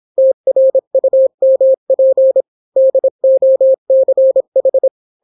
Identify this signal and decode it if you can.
Morse code, heard as a keyed tone, about 26 wpm, 530 Hz. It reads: TRUMP DOCH